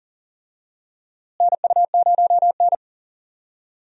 Morse code, heard as a keyed tone, about 40 wpm, 700 Hz. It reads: NU0N
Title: USA Call Signs - 4 Characters - Repeated 2x - 40wpm